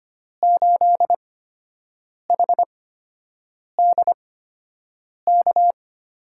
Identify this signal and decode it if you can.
Morse code, heard as a keyed tone, about 25 wpm, 700 Hz. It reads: 8 H D K
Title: USA Call Signs - 8x Character Spacing - 25wpm